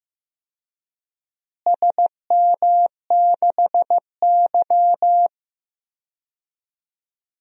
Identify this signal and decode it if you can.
Morse code, heard as a keyed tone, about 15 wpm, 700 Hz. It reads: SM6Y
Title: DXCC-100 Call Signs - 4 Characters - Repeated 3x - 15wpm